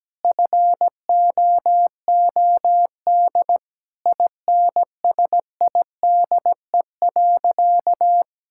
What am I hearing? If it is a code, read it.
Morse code, heard as a keyed tone, about 17 wpm, 700 Hz. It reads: FOOD INSIDE.
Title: Sentences from Top 1000 Words - Review Part 6 - 17wpm